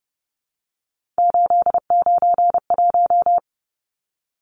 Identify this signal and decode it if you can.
Morse code, heard as a keyed tone, about 30 wpm, 700 Hz. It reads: 891
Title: Numbers - 3-Digit 30wpm